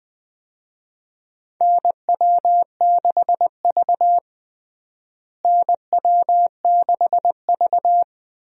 Morse code, heard as a keyed tone, about 20 wpm, 700 Hz. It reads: NW6V NW6V
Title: USA Call Signs - 4 Characters - Repeated 3x - 20wpm